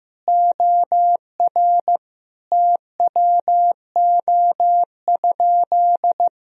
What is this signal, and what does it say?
Morse code, heard as a keyed tone, about 15 wpm, 700 Hz. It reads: OR TWO?
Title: Sentences from Top 1000 Words - Review Part 4 - 15wpm